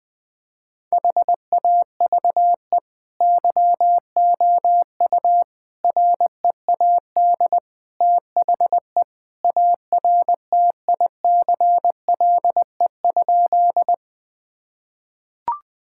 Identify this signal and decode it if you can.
Morse code, heard as a keyed tone, about 20 wpm, 700 Hz. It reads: HAVE YOU READ THE ARTICLE?
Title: Sentences from Top 1000 Words - Review Part 5 - 20wpm